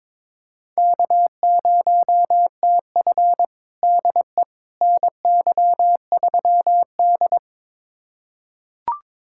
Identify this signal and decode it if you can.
Morse code, heard as a keyed tone, about 22 wpm, 700 Hz. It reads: K0TF DE NY3D
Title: USA Callsign DE Callsign - 4 Characters - 22wpm